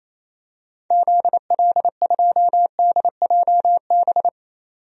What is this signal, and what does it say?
Morse code, heard as a keyed tone, about 28 wpm, 700 Hz. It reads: ZL2DJB
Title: DXCC-100 Call Signs - 6 Characters - 28wpm